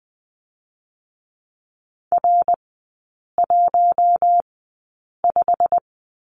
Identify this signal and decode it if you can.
Morse code, heard as a keyed tone, about 20 wpm, spaced as for 8 wpm, 700 Hz. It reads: R15
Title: 087 Lesson 15e - Number 5 - Three Characters 20wpm